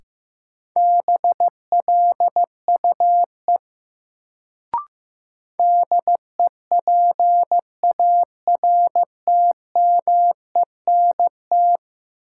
Morse code, heard as a keyed tone, about 15 wpm, 700 Hz. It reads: BLUE DEPARTMENT
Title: Sentences from Top 1200 Words - New Word Drill Alt - 15wpm